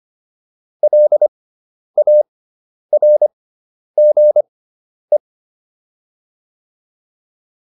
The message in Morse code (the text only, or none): LARGE